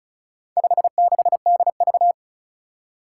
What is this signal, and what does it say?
Morse code, heard as a keyed tone, about 36 wpm, 700 Hz. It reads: 56DV
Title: Four Challenging Letter-Number Combinations 35wpm